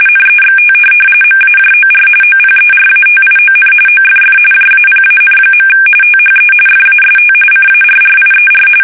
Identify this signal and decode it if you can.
RTTY, about 45.45 baud. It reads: NEXT MAJOR OPERATION OFF JAPAN. THE FAST CARRIER
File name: fsk850 / USS-Massachusetts